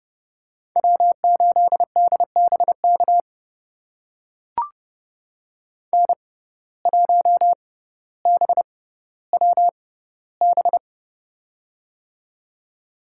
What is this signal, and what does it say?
Morse code, heard as a keyed tone, about 30 wpm, spaced as for 15 wpm, 700 Hz. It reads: W8DBK N1BWB
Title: USA Call Signs - 5 Characters - 6x Character Spacing - 30wpm